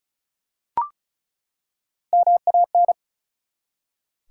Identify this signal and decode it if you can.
Morse code, heard as a keyed tone, about 36 wpm, 700 Hz. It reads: MAN
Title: Three Letter Words - 35wpm